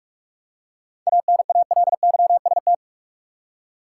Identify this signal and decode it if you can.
Morse code, heard as a keyed tone, about 44 wpm, 700 Hz. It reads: ANALYST
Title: Seven Letter Words - 45wpm